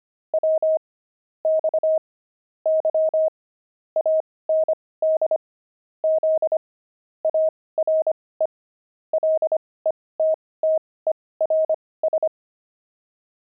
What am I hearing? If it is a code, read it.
Morse code, heard as a keyed tone, about 25 wpm, spaced as for 18 wpm, 630 Hz. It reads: W X Y AND Z ARE LETTERS
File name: pangram-fs-18-cs-25-letters